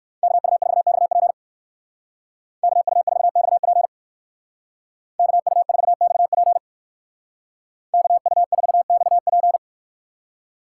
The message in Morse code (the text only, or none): KU4XP KU4XP KU4XP KU4XP